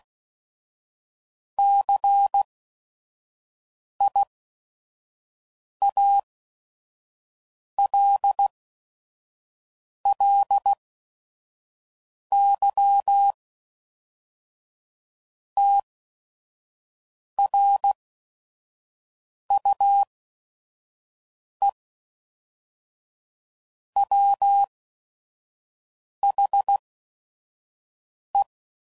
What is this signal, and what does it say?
Morse code, heard as a keyed tone, about 16 wpm, spaced as for 5 wpm, 800 Hz. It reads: CIALLY TRUE WHE
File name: CW Code Groups @5WPM / Technical Subjects / AND Gate